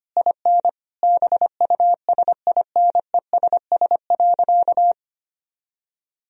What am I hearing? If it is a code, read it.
Morse code, heard as a keyed tone, about 25 wpm, 700 Hz. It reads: IN BUSINESS.
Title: Sentences from Top 1000 Words - Review Part 4 - 25wpm